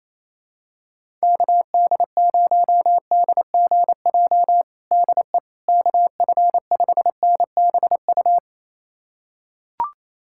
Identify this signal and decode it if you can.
Morse code, heard as a keyed tone, about 28 wpm, 700 Hz. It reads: KD0DGJ DE KF5NBU
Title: USA Callsign DE Callsign - 6 Characters - 28wpm